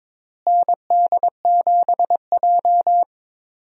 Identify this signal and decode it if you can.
Morse code, heard as a keyed tone, about 22 wpm, 700 Hz. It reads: ND7J
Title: USA Call Signs - 4 Characters - Repeated 3x - 22wpm